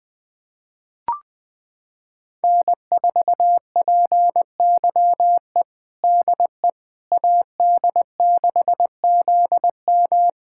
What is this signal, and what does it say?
Morse code, heard as a keyed tone, about 20 wpm, 700 Hz. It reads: N4PYE DE AD6ZM
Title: USA Callsign DE Callsign - 5 Characters - 20wpm